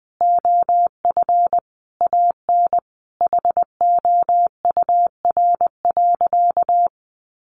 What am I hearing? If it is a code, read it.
Morse code, heard as a keyed tone, about 20 wpm, 700 Hz. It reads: OF AN HOUR.
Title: Sentences from Top 1500 Words 20wpm